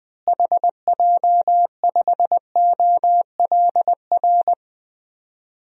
Morse code, heard as a keyed tone, about 21 wpm, 700 Hz. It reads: HJ5OLR
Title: DXCC-100 Call Signs - 6 Characters - Repeated 3x - 20wpm